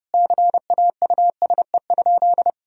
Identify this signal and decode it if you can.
Morse code, heard as a keyed tone, about 32 wpm, 700 Hz. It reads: CAUSE?
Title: Sentences from Top 1000 Words - Review Part 1 - 30wpm